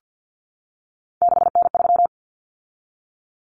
Morse code, heard as a keyed tone, about 50 wpm, 700 Hz. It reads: 6N3
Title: Common Cut Numbers - 3-Digit 50wpm